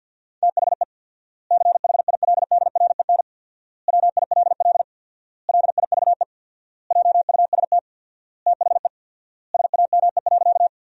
Morse code, heard as a keyed tone, about 50 wpm, 700 Hz. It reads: THE CHILDREN WILL LIVE JUST THE SAME.
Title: 4x Word-Spacing - Sentences from Top 300 Words - 50wpm